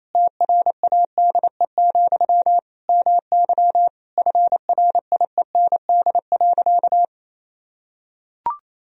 Morse code, heard as a keyed tone, about 28 wpm, 700 Hz. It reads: TRADE, MY FRIEND.
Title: Sentences from Top 1000 Words - Review Part 4 - 28wpm